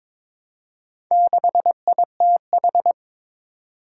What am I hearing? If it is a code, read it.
Morse code, heard as a keyed tone, about 22 wpm, 700 Hz. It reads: 6ITH
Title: Four Challenging Letter-Number Combinations 22wpm